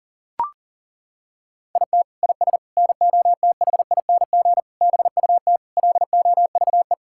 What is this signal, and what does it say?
Morse code, heard as a keyed tone, about 40 wpm, 700 Hz. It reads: IT IS NOTHING BUT LOVE
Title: Sentences from Top 1600 Words 40wpm